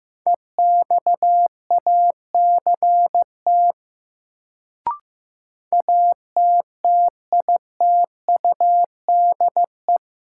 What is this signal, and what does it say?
Morse code, heard as a keyed tone, about 15 wpm, 700 Hz. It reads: EXACT ATTITUDE
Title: Sentences from Top 1300 Words - New Word Drill Alt - 15wpm